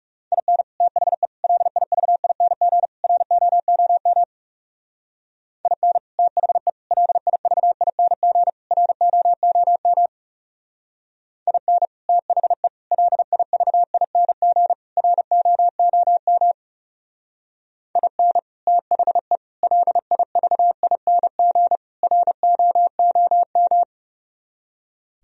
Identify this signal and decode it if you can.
Morse code, heard as a keyed tone, about 37 wpm, 700 Hz. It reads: IN THE LIVING ROOM IN THE LIVING ROOM IN THE LIVING ROOM IN THE LIVING ROOM
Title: Speed-Racing - Sets of 4 Words 30wpm